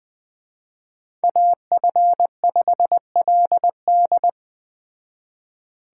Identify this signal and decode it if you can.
Morse code, heard as a keyed tone, about 20 wpm, 700 Hz. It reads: AF5LD